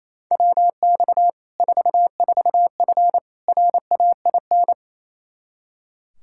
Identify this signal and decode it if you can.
Morse code, heard as a keyed tone, about 28 wpm, 700 Hz. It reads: WX 44F RAIN
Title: Practice QSO - Exchange 2 - Info - 28wpm